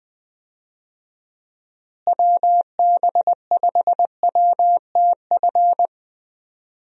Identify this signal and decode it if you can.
Morse code, heard as a keyed tone, about 21 wpm, 700 Hz. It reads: WB5WTF
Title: USA Call Signs - 6 Characters - Repeated 2x - 20wpm